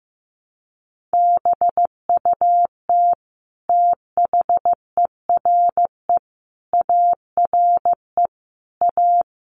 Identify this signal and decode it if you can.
Morse code, heard as a keyed tone, about 15 wpm, 700 Hz. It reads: BUT THERE ARE A
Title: Sets of 5 Words Encore2 15wpm